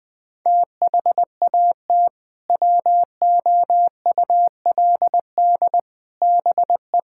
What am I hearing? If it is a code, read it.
Morse code, heard as a keyed tone, about 20 wpm, 700 Hz. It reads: THAT WOULD BE
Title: Sentences with 2-Word Phrases - Spoken Beginning - 20wpm